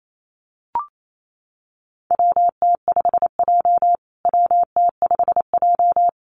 Morse code, heard as a keyed tone, about 28 wpm, 700 Hz. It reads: WT5J WT5J
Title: USA Call Signs Repeated 2x 28wpm